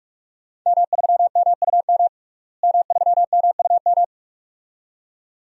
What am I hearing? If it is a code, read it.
Morse code, heard as a keyed tone, about 43 wpm, 700 Hz. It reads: M3MUM M3MUM
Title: DXCC-100 Call Signs - 5 Characters - Repeated 2x - 45wpm